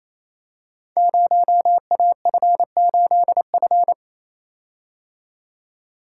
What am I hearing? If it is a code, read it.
Morse code, heard as a keyed tone, about 29 wpm, 700 Hz. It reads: 0AF8F
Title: Code Groups - 5 Letter:Numbers - 28wpm